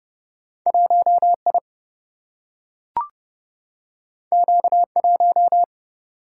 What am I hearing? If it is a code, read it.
Morse code, heard as a keyed tone, about 30 wpm, 700 Hz. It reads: 1I Q1
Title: Single-Letter Single-Number 30wpm